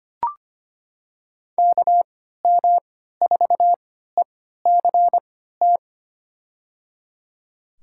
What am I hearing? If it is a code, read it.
Morse code, heard as a keyed tone, about 25 wpm, spaced as for 14 wpm, 700 Hz. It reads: KM4ECT
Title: USA Call Signs - 3x Character Spacing - 25wpm